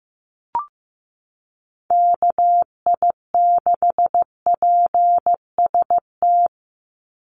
Morse code, heard as a keyed tone, about 15 wpm, 700 Hz. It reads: KI6PST
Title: USA Call Signs 15wpm